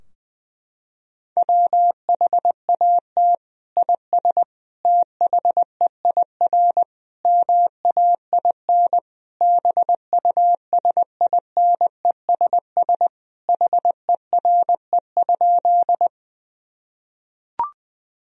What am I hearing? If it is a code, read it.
Morse code, heard as a keyed tone, about 20 wpm, 700 Hz. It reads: WHAT IS THEIR MAIN BUSINESS HERE?
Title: Sentences from Top 1000 Words - Review Part 2 - 20wpm